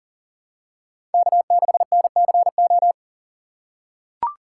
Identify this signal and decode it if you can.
Morse code, heard as a keyed tone, about 40 wpm, 700 Hz. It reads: K6NCO
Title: USA Call Signs - 5 Characters - 2x Character Spacing - 40wpm